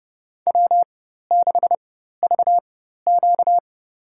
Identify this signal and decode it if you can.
Morse code, heard as a keyed tone, about 32 wpm, spaced as for 14 wpm, 700 Hz. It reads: W6VQ
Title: USA Call Signs - 4 Characters - 4x Character Spacing - 30wpm